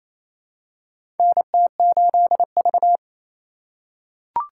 Morse code, heard as a keyed tone, about 28 wpm, 700 Hz. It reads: NT8V
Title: USA Call Signs - 4 Characters - 2x Character Spacing - 28wpm